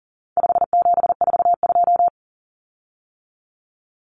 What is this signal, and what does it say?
Morse code, heard as a keyed tone, about 40 wpm, 700 Hz. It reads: 5742
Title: Common Cut Numbers - 4-Digit 40wpm